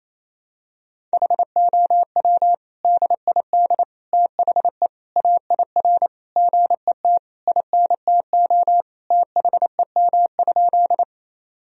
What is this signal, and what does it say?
Morse code, heard as a keyed tone, about 27 wpm, 700 Hz. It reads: HOW DID THE AIR GET INTO THEM?
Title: Sentences from Top 1000 Words - Review Part 1 - 28wpm